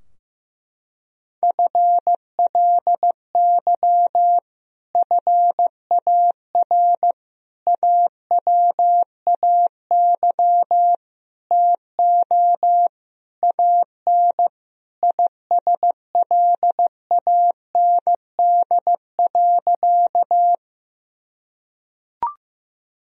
Morse code, heard as a keyed tone, about 15 wpm, 700 Hz. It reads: FLY FAR AWAY TO AN ISLAND.
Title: Sentences from Top 1000 Words - Review Part 2 - 15wpm